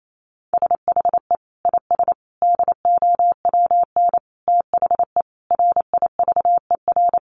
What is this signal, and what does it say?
Morse code, heard as a keyed tone, about 28 wpm, 700 Hz. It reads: SHE IS DOWN THE RIVER
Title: Sentences from Top 1400 Words 28wpm